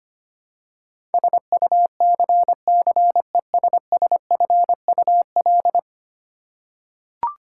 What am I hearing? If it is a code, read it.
Morse code, heard as a keyed tone, about 25 wpm, 700 Hz. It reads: SUCCESSFUL
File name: 10 Letter Words - First 3 Letters Given - 25wpm